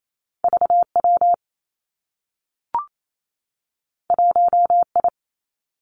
Morse code, heard as a keyed tone, about 28 wpm, 700 Hz. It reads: V W 1 I
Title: Two Challenging Letter-Number Combinations 28wpm